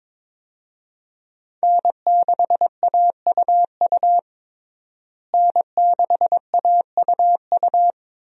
Morse code, heard as a keyed tone, about 22 wpm, 700 Hz. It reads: N6AUU N6AUU
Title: USA Call Signs - 5 Characters - Repeated 3x - 22wpm